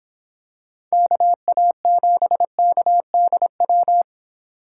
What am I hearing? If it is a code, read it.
Morse code, heard as a keyed tone, about 26 wpm, 700 Hz. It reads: KA7KDW